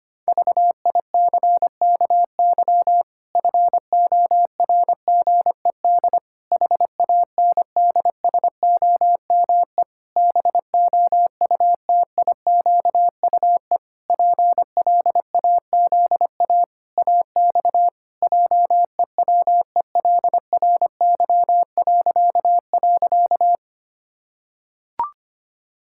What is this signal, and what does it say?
Morse code, heard as a keyed tone, about 25 wpm, 700 Hz. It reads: VICKY FORGED HANDSOME BOUTIQUE PLAZA AX JEWELRY..
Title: Pangram Challenge v2 - 25wpm